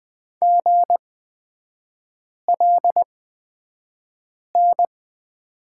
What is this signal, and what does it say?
Morse code, heard as a keyed tone, about 20 wpm, spaced as for 5 wpm, 700 Hz. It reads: GLN
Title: 224 Lesson 36e - BT - Three Characters 20wpm